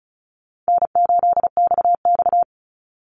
Morse code, heard as a keyed tone, about 36 wpm, 700 Hz. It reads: N8XX
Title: Call Signs - 35wpm